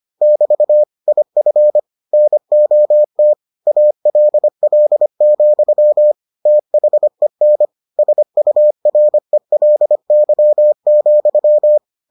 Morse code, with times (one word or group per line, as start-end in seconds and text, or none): 0.2-0.8 <BT>
1.1-1.8 IF
2.1-3.3 NOT
3.7-6.1 ALL,
6.5-7.7 THEN
8.0-11.8 SURELY,